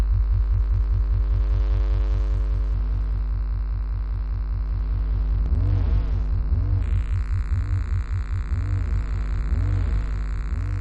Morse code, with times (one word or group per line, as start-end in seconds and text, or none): none